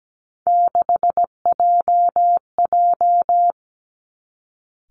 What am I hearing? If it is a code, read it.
Morse code, heard as a keyed tone, about 17 wpm, 700 Hz. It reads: 6JJ
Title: Three Challenging Letter-Number Combinations 17wpm